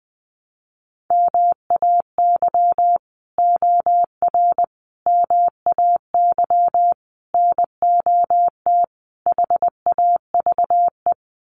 Morse code, heard as a keyed tone, about 20 wpm, 700 Hz. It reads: MAY OR MAY NOT HAVE
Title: Sets of 5 Words Encore2 20wpm